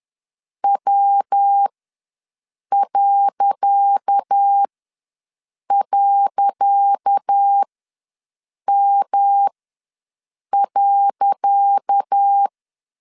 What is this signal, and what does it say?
Morse code, heard as a keyed tone, about 10 wpm, spaced as for 6 wpm, 800 Hz. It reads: W..M.
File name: LEZIONI / 10 SIGNS / 10C DOT ws 10wpm 4213